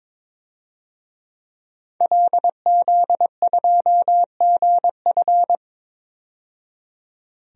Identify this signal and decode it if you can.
Morse code, heard as a keyed tone, about 22 wpm, 700 Hz. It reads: LZ2GF
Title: DXCC-100 Call Signs - 5 Characters - 22wpm